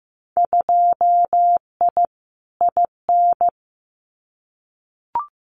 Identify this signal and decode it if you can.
Morse code, heard as a keyed tone, about 15 wpm, 700 Hz. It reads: 2I IN
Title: Winter Field Day Exchanges Encore 15wpm